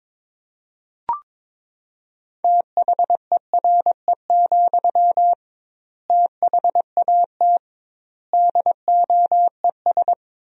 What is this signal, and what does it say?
Morse code, heard as a keyed tone, about 22 wpm, 700 Hz. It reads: THERE, THAT DOES
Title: 2x Word-Spacing - Sentences from Top 200 Words - 22wpm